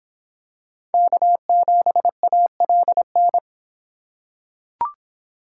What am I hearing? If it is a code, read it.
Morse code, heard as a keyed tone, about 25 wpm, 700 Hz. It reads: K7ALN